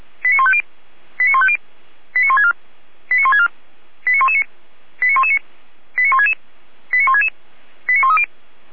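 Five-tone selective call (ZVEI), heard as quick runs of five tones, tones of about 70 ms. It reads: 89150 89150 89164 89164 89119 89119 89166 89166 89130
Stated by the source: ZVEI five-tone selective call, address 89158, received on an SDR.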